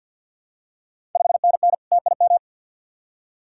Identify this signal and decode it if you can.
Morse code, heard as a keyed tone, about 47 wpm, 690 Hz. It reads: 5NN TIM
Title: Ontario QSO Party Exchange 50wpm